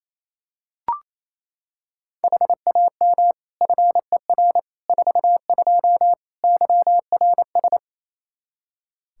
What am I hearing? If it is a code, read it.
Morse code, heard as a keyed tone, about 28 wpm, 700 Hz. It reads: HAM FER 42 YRS
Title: Practice QSO - Exchange 3 - More Info - 28wpm